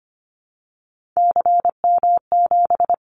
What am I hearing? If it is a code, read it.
Morse code, heard as a keyed tone, about 25 wpm, 700 Hz. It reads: CM7
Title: Callsign Prefixes for Top 100 DXCC - 25wpm